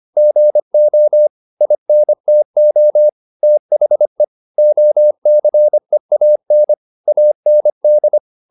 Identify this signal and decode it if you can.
Morse code, heard as a keyed tone, about 25 wpm, 600 Hz. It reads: GO INTO THE OCEAN AND